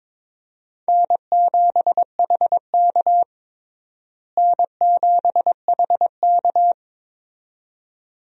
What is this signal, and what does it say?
Morse code, heard as a keyed tone, about 22 wpm, 700 Hz. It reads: N7HK N7HK
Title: USA Call Signs - 4 Characters - Repeated 2x - 22wpm